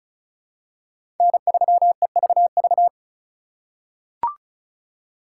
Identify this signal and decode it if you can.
Morse code, heard as a keyed tone, about 36 wpm, 700 Hz. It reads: N3EVV
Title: USA Call Signs - 5 Characters - 3x Character Spacing - 35wpm